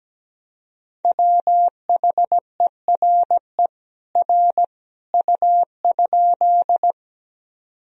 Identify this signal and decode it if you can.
Morse code, heard as a keyed tone, about 17 wpm, 700 Hz. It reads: WHERE R U?
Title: Sentences from Top 200 Words and Abbreviations 17wpm